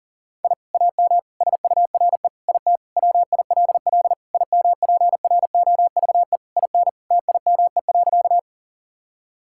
Morse code, heard as a keyed tone, about 40 wpm, 700 Hz. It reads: I AM SURE IT WILL IMPROVE IN TIME.
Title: Sentences from Top 1000 Words - Review Part 5 - 40wpm